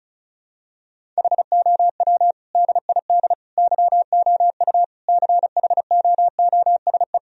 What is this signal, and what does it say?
Morse code, heard as a keyed tone, about 34 wpm, 700 Hz. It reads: HOW DID YOU CHOOSE
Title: Sentences from Top 1000 Words - Review Part 4 - 35wpm